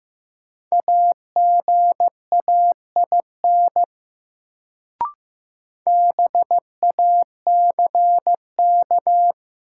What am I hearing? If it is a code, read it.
Morse code, heard as a keyed tone, about 15 wpm, 700 Hz. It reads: AGAIN BACK
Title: Sentences from Top 200 Words - New Word Drill - 15wpm